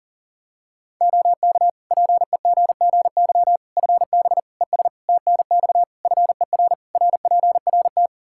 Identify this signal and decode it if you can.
Morse code, heard as a keyed tone, about 37 wpm, 700 Hz. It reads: OK PEGGY FB ES TNX FER RPRT
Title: Practice QSO - Exchange 2 - Info - 40wpm